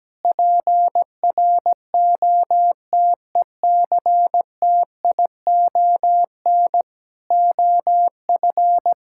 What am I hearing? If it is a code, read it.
Morse code, heard as a keyed tone, about 17 wpm, 700 Hz. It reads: PROTECTION OF
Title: Sentences from Top 1600 Words 17wpm